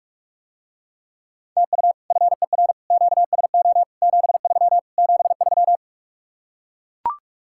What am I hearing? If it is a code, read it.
Morse code, heard as a keyed tone, about 44 wpm, 700 Hz. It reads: TU FER QSO 73 73